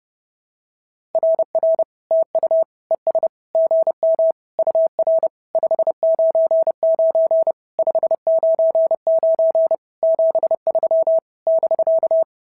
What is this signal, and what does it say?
Morse code, heard as a keyed tone, about 30 wpm, 650 Hz. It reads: RR TU ES GM UR 599 599 73 <BK>
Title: SOTA QSO Practice 30wpm